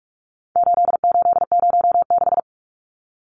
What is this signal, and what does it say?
Morse code, heard as a keyed tone, about 44 wpm, 700 Hz. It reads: 8806
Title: Numbers - 4-Digit 45wpm